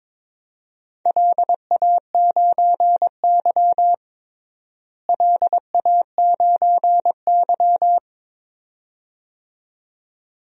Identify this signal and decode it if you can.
Morse code, heard as a keyed tone, about 22 wpm, 700 Hz. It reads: LA9Y LA9Y
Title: DXCC-100 Call Signs - 4 Characters - Repeated 2x - 22wpm